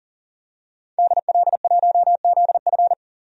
Morse code, heard as a keyed tone, about 39 wpm, 700 Hz. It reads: DL1ZF
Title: DXCC-100 Call Signs - 5 Characters - 40wpm